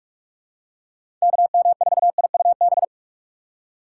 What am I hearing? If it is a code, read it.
Morse code, heard as a keyed tone, about 42 wpm, 690 Hz. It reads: KM4IUB
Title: USA Call Signs - 6 Characters - 2x Character Spacing - 45wpm